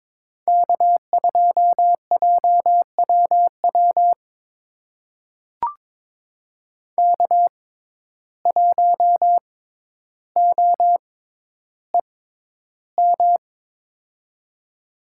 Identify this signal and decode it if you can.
Morse code, heard as a keyed tone, about 23 wpm, spaced as for 11 wpm, 700 Hz. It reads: K2JWW K1OEM
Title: USA Call Signs - 5 Characters - 6x Character Spacing - 22wpm